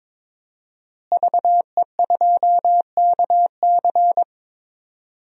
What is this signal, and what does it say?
Morse code, heard as a keyed tone, about 22 wpm, 700 Hz. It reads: VE2KC
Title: DXCC-100 Call Signs - 5 Characters - Repeated 3x - 22wpm